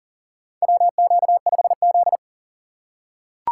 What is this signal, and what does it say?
Morse code, heard as a keyed tone, about 40 wpm, 700 Hz. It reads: WQ5Z
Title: USA Call Signs - 4 Characters - 6x Character Spacing - 40wpm